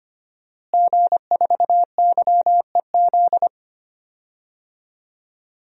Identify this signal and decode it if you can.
Morse code, heard as a keyed tone, about 25 wpm, 700 Hz. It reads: G4YEZ
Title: DXCC-100 Call Signs - 5 Characters - Repeated 3x - 25wpm